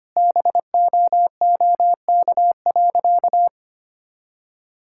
Morse code, heard as a keyed tone, about 25 wpm, 690 Hz. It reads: BOOK.
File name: practice / 25wpm / Sentences with 2-Word Phrases - 25wpm